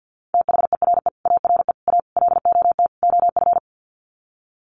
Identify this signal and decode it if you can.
Morse code, heard as a keyed tone, about 50 wpm, 700 Hz. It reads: THERE ARE A LOT OF
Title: Sets of 5 Words 50wpm